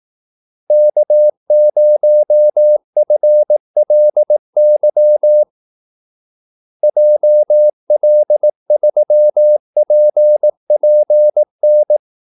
Morse code, heard as a keyed tone, about 18 wpm, 600 Hz. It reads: K0FLY JL3PPN
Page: Kaldesignaler 18 wpm